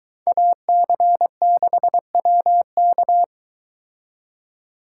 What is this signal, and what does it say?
Morse code, heard as a keyed tone, about 22 wpm, 700 Hz. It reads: AC6WK